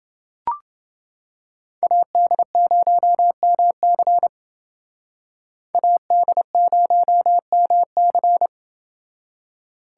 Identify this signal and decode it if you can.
Morse code, heard as a keyed tone, about 30 wpm, 700 Hz. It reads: AD0MC AD0MC